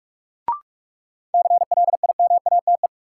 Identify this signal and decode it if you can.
Morse code, heard as a keyed tone, about 46 wpm, 700 Hz. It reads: CLIMATE
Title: Sentences from Top 1600 Words - New Word Drill Alt - 45wpm